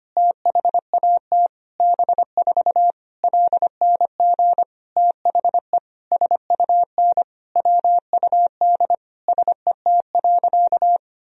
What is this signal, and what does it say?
Morse code, heard as a keyed tone, about 25 wpm, 700 Hz. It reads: THAT B4 LNG THE SUN WUD SET.
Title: Sentences from Top 400 Words and Abbreviations 25wpm